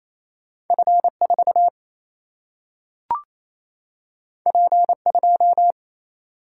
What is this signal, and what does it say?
Morse code, heard as a keyed tone, about 28 wpm, 700 Hz. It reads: F 4 P 2
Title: Single-Letter Single-Number 28wpm